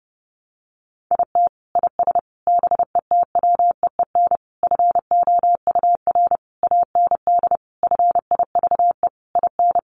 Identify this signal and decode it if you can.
Morse code, heard as a keyed tone, about 30 wpm, 700 Hz. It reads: IT IS BETWEEN FOUR AND FIVE IN
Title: Sentences from Top 1500 Words 30wpm